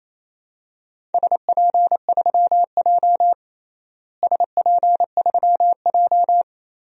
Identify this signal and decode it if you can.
Morse code, heard as a keyed tone, about 28 wpm, 700 Hz. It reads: SP3J SP3J
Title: DXCC-100 Call Signs - 4 Characters - Repeated 3x - 28wpm